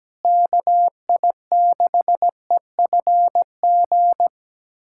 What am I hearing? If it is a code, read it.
Morse code, heard as a keyed tone, about 17 wpm, 700 Hz. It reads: KI6EFG
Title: USA Callsign DE Callsign 17wpm